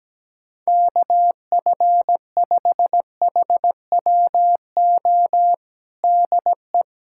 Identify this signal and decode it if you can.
Morse code, heard as a keyed tone, about 17 wpm, 700 Hz. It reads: KF5HWO DE
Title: USA Callsign DE Callsign - 6 Characters - 17wpm